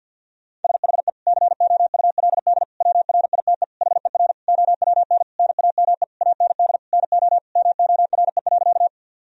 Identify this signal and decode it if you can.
Morse code, heard as a keyed tone, about 52 wpm, 700 Hz. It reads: SHE COULD WRITE HER OWN NAME AND NO MORE.
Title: Sentences from Top 1000 Words - Review Part 1 - 50wpm